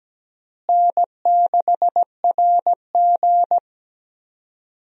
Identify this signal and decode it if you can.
Morse code, heard as a keyed tone, about 17 wpm, 700 Hz. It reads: N6RG
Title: USA Call Signs - 4 Characters - Repeated 2x - 17wpm